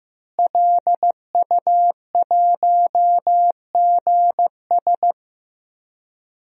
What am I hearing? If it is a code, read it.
Morse code, heard as a keyed tone, about 16 wpm, 700 Hz. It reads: LU1GS
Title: DXCC-100 Call Signs - 5 Characters - Repeated 3x - 15wpm